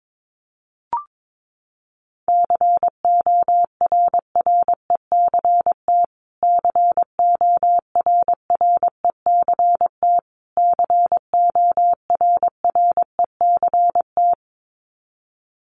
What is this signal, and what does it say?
Morse code, heard as a keyed tone, about 22 wpm, 700 Hz. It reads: CORRECT CORRECT CORRECT
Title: Top 500 Words Repeated 22wpm